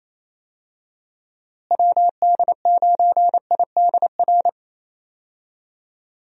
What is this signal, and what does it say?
Morse code, heard as a keyed tone, about 28 wpm, 700 Hz. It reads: WD9IDR
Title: USA Call Signs - 6 Characters - Repeated 2x - 28wpm